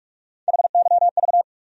Morse code, heard as a keyed tone, about 45 wpm, 700 Hz. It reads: HYV
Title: USA Call Sign Postfixes - 3 Letters 45wpm